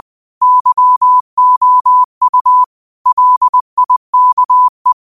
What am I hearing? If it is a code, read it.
Morse code, heard as a keyed tone, about 20 wpm, 1000 Hz. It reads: YOU LIKE
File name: qotdfiles / 20wpm